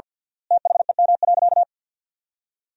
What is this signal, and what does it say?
Morse code, heard as a keyed tone, about 50 wpm, 700 Hz. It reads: THEM.
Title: 3x Word-Spacing - Sentences from Top 400 Words - 50wpm